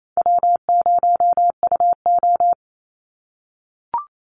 Morse code, heard as a keyed tone, about 28 wpm, 700 Hz. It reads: W0UO
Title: Call Signs - 28wpm